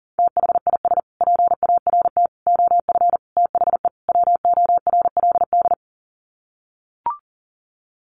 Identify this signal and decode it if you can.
Morse code, heard as a keyed tone, about 40 wpm, 700 Hz. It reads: THIS PART OF THE WORLD
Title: Sets of 5 Words Encore2 40wpm